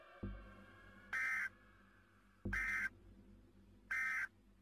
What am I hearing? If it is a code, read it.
Emergency Alert System (SAME) tones, heard as harsh buzzing bursts. It is heard from 1.1 to 4.2 s.